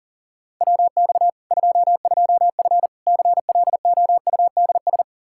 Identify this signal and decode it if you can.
Morse code, heard as a keyed tone, about 40 wpm, 700 Hz. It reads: WX 22F CLOUDS
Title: Practice QSO - Exchange 2 - Info - 40wpm